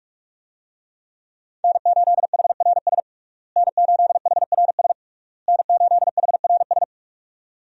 Morse code, heard as a keyed tone, about 41 wpm, 700 Hz. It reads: N8HRS N8HRS N8HRS
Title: USA Call Signs - 5 Characters - Repeated 3x - 45wpm